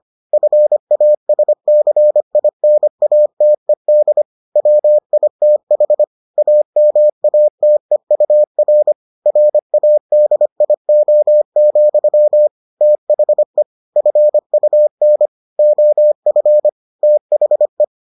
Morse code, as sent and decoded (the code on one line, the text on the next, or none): ..-. .- ... -.-. .. -. .- - . -.. / .-- .. - .... / .- -- .- - . ..- .-. / .-. .- -.. .. --- --..-- / - .... / ..-. ..- -. / --- ..-. / - .... .
FASCINATED WITH AMATEUR RADIO, THE FUN OF THE